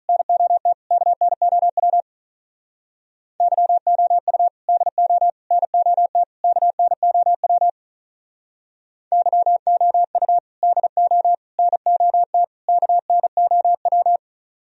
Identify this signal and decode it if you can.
Morse code, heard as a keyed tone, about 43 wpm, 700 Hz. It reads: NOT KNOW YOU DO NOT KNOW YOU DO NOT KNOW
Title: Speed-Racing - Sets of 4 Words 35wpm